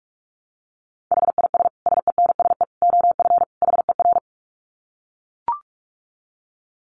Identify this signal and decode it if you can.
Morse code, heard as a keyed tone, about 44 wpm, 700 Hz. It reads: HIS SENSE OF HER
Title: Dit Challenge 45wpm